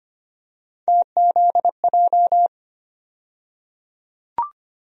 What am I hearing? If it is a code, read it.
Morse code, heard as a keyed tone, about 25 wpm, 700 Hz. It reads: TZJ
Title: USA Call Sign Postfixes - 3 Letters 25wpm